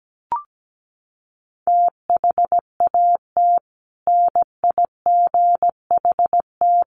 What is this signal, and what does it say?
Morse code, heard as a keyed tone, about 17 wpm, 700 Hz. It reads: THAT NIGHT
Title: Sentences from Top 300 Words 17wpm